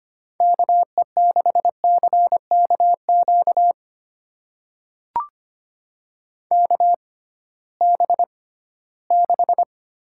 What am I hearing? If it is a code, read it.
Morse code, heard as a keyed tone, about 25 wpm, spaced as for 16 wpm, 700 Hz. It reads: KE6CKQ KB6
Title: USA Call Signs - 6 Characters - 6x Character Spacing - 25wpm